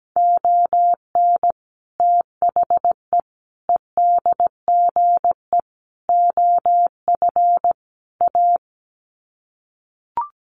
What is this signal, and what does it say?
Morse code, heard as a keyed tone, about 17 wpm, 700 Hz. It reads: ON THE EDGE OF A
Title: Sets of 5 Words Encore2 17wpm